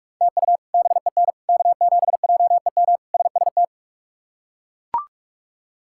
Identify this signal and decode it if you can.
Morse code, heard as a keyed tone, about 41 wpm, 700 Hz. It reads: TU BEN K7JEM SST
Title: K1USN SST QSO Practice - 45wpm